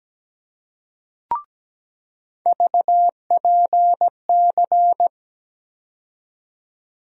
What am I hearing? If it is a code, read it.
Morse code, heard as a keyed tone, about 17 wpm, 700 Hz. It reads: VPC
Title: USA Call Sign Postfixes - 3 Letters 17wpm